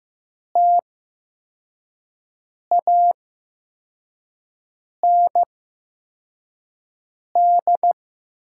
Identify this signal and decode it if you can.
Morse code, heard as a keyed tone, about 15 wpm, spaced as for 4 wpm, 700 Hz. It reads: TAND
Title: 8x Character Spacing - Top 500 Words - 15wpm